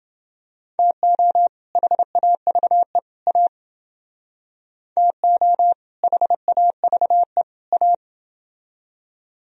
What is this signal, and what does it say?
Morse code, heard as a keyed tone, about 30 wpm, 700 Hz. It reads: TO HAVE A TO HAVE A